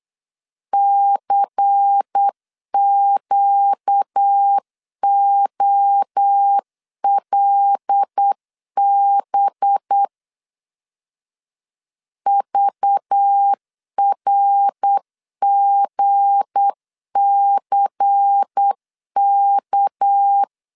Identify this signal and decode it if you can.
Morse code, heard as a keyed tone, about 8 wpm, 800 Hz. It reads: CQOLB VRGCK
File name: LEZIONI / 7 GF9Z / 7 G 8wpm 3118